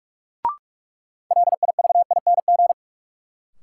Morse code, heard as a keyed tone, about 46 wpm, 700 Hz. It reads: LIVING
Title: Sentences from Top 1000 Words - Review Part 6 - New Word Drill - 45wpm